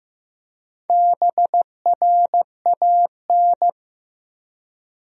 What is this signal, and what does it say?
Morse code, heard as a keyed tone, about 15 wpm, 700 Hz. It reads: BRAN
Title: USA Common Names - Max 5 Letters - 15wpm